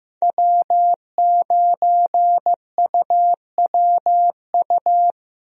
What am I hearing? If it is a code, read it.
Morse code, heard as a keyed tone, about 15 wpm, 700 Hz. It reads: W9UWU
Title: USA Call Signs - 5 Characters - Repeated 2x - 15wpm